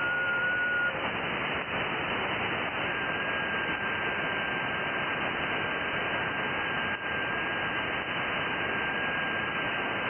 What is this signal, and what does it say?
FT8.
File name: IC-7100